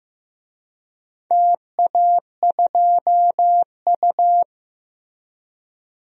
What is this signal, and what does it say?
Morse code, heard as a keyed tone, about 15 wpm, 700 Hz. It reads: TA2U
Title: DXCC-100 Call Signs - 4 Characters - 15wpm